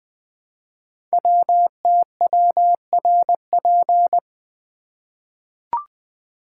Morse code, heard as a keyed tone, about 20 wpm, 700 Hz. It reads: WTWRP
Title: Code Groups - 5 Letters - 20wpm